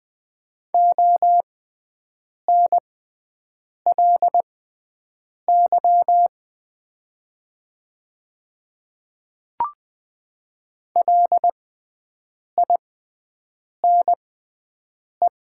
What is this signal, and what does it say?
Morse code, heard as a keyed tone, about 20 wpm, spaced as for 7 wpm, 700 Hz. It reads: ONLY LINE
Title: 6x Character Spacing - Top 300 Words - 20wpm